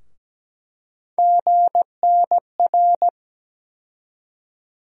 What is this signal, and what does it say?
Morse code, heard as a keyed tone, about 17 wpm, 700 Hz. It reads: GNR
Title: USA Call Sign Postfixes - 3 Letters 17wpm